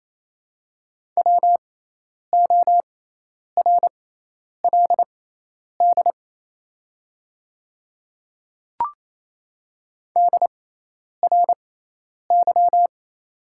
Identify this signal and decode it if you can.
Morse code, heard as a keyed tone, about 28 wpm, spaced as for 10 wpm, 700 Hz. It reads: WORLD DRY
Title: 6x Character Spacing - Top 500 Words - 28wpm